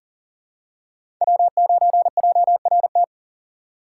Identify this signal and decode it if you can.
Morse code, heard as a keyed tone, about 40 wpm, 700 Hz. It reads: W9JRT
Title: USA Call Signs - 4x Character Spacing - 40wpm